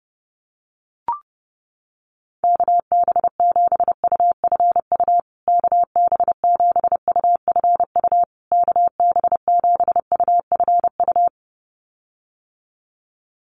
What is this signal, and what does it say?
Morse code, heard as a keyed tone, about 30 wpm, 700 Hz. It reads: KB7UFU KB7UFU KB7UFU
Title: USA Call Signs Repeated 3x 30wpm